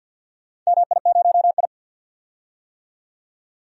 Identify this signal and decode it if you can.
Morse code, heard as a keyed tone, about 49 wpm, 700 Hz. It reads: MI0I
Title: DXCC-100 Call Signs - 4 Characters - 50wpm